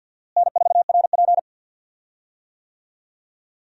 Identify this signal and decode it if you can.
Morse code, heard as a keyed tone, about 51 wpm, 700 Hz. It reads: N4RP
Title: USA Call Signs - 4 Characters - Repeated 3x - 50wpm